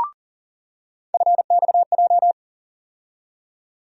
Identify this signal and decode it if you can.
Morse code, heard as a keyed tone, about 39 wpm, 700 Hz. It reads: FXJ
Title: USA Call Sign Postfixes - 3 Letters 40wpm